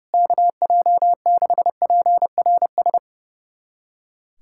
Morse code, heard as a keyed tone, about 30 wpm, 700 Hz. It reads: KJ6PRS
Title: USA Callsign DE Callsign 30wpm